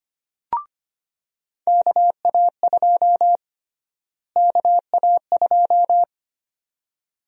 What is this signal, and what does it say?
Morse code, heard as a keyed tone, about 25 wpm, 700 Hz. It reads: KA2 KA2
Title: USA Call Sign Prefixes - Repeated 2x - 25wpm